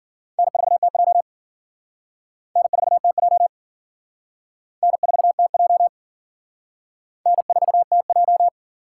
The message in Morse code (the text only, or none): N4TJ N4TJ N4TJ N4TJ